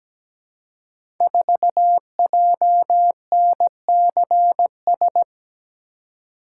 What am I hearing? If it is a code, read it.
Morse code, heard as a keyed tone, about 17 wpm, 700 Hz. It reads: 4JNCS
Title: Code Groups - 5 Letter:Numbers - 17wpm